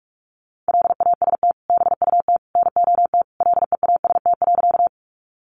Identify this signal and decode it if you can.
Morse code, heard as a keyed tone, about 44 wpm, 700 Hz. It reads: LAST BUT NOT LEAST.
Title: Sentences from Top 700 Words 45wpm